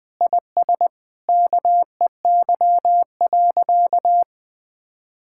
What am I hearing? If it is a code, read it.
Morse code, heard as a keyed tone, about 20 wpm, 700 Hz. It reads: IS KEY.
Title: Sentences with 7-Word Phrases - 20wpm